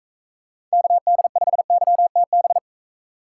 Morse code, heard as a keyed tone, about 41 wpm, 700 Hz. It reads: KD5YTB